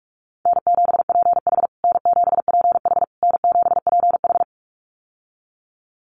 Morse code, heard as a keyed tone, about 44 wpm, 700 Hz. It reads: N7PH N7PH N7PH
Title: USA Call Signs Repeated 3x 45wpm